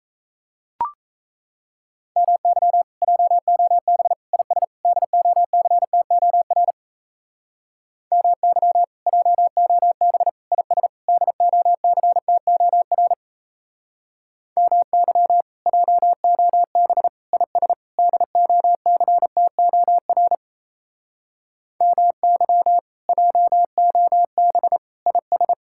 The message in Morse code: MY JOB IS DOCTOR MY JOB IS DOCTOR MY JOB IS DOCTOR MY JOB IS

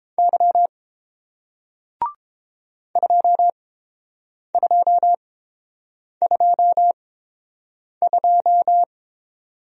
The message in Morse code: Y 2222

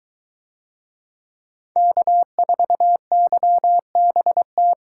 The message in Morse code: K4YBT

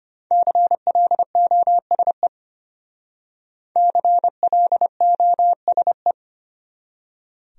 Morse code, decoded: CLOSE CLOSE

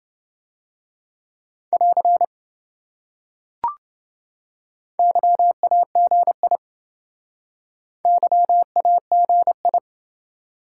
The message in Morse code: <AR> YAGI YAGI

